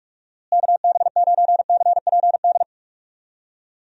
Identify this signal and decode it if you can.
Morse code, heard as a keyed tone, about 46 wpm, 700 Hz. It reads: KB9CPD
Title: USA Call Signs - 6 Characters - Repeated 3x - 45wpm